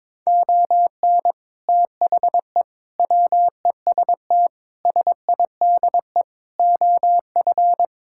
Morse code, heard as a keyed tone, about 22 wpm, 700 Hz. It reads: ON THE WEST SIDE OF
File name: practice / 22wpm / Sets of 6 Words - 22wpm